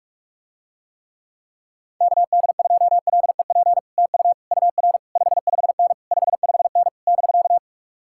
Morse code, heard as a keyed tone, about 49 wpm, 700 Hz. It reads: KD2LEP TU UR 55N 55N <BK>